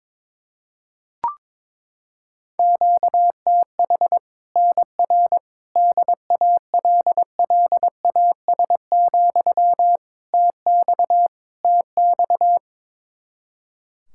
Morse code, heard as a keyed tone, about 21 wpm, 700 Hz. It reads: QTH NR DALLAS, TX TX
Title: Practice QSO - Exchange 1 - Beginning - 22wpm